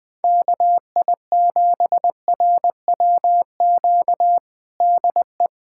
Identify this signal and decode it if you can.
Morse code, heard as a keyed tone, about 20 wpm, 700 Hz. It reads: KI7RWQ DE
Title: USA Callsign DE Callsign 20wpm